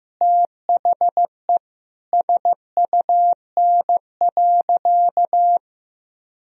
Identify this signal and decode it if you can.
Morse code, heard as a keyed tone, about 15 wpm, 700 Hz. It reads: THE SUN.
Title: Sentences from Top 1000 Words - Review Part 1 - 15wpm